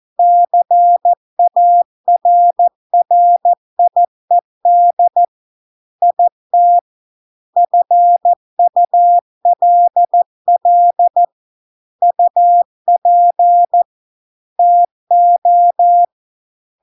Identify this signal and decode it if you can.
Morse code, heard as a keyed tone, about 14 wpm, 700 Hz. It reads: CARRIED IT FULL UP TO